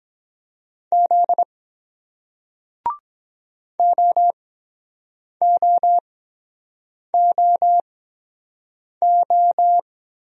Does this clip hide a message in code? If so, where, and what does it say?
Yes: Morse code from 0.9 to 9.8 s, heard as a keyed tone, about 23 wpm, spaced as for 7 wpm, 700 Hz. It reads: Z OOOO